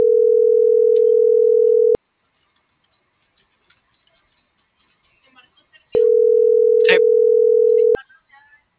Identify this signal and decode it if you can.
Ringback tone (North America), 440 + 480 Hz.